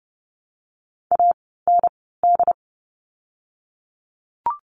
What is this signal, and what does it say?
Morse code, heard as a keyed tone, about 30 wpm, spaced as for 17 wpm, 700 Hz. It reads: AND
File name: Lesson 17f - Letter u - Words 30wpm